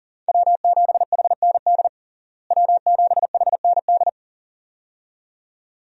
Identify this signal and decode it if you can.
Morse code, heard as a keyed tone, about 40 wpm, 700 Hz. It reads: W7HND W7HND
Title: USA Call Signs - 5 Characters - Repeated 2x - 40wpm